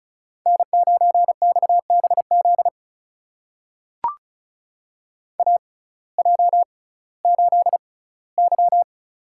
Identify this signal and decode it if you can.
Morse code, heard as a keyed tone, about 36 wpm, spaced as for 19 wpm, 700 Hz. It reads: N9XBZ AJ8Y
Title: USA Call Signs - 6x Character Spacing - 35wpm